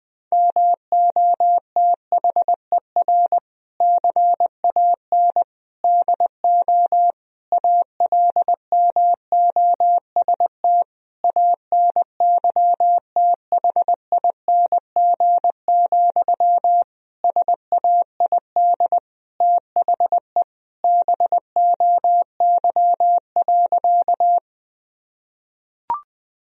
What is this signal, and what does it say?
Morse code, heard as a keyed tone, about 20 wpm, 700 Hz. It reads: MOTHER CAN DO ALMOST ANYTHING, SAID THE BOY.
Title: Sentences from Top 1000 Words - Review Part 4 - 20wpm